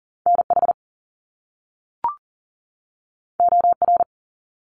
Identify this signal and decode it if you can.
Morse code, heard as a keyed tone, about 39 wpm, 700 Hz. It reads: NH OR
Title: US States 40wpm